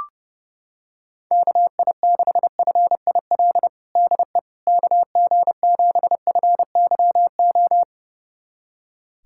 Morse code, heard as a keyed tone, about 29 wpm, 700 Hz. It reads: KI6FIL DE KG7FYO
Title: USA Callsign DE Callsign 30wpm